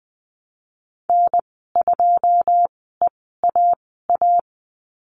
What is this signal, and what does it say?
Morse code, heard as a keyed tone, about 20 wpm, spaced as for 15 wpm, 700 Hz. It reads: N2EAA
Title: 161 Lesson 26g - Letter g - Call Signs 20wpm